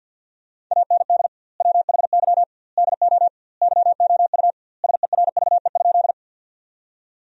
Morse code, heard as a keyed tone, about 50 wpm, 700 Hz. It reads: AND WHY DO YOU SERVE?
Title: 2x Word-Spacing - Sentences from Top 400 Words - 50wpm